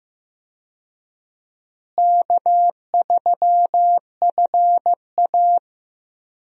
Morse code, heard as a keyed tone, about 16 wpm, 700 Hz. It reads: K3FA
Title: USA Call Signs - 4 Characters - Repeated 3x - 15wpm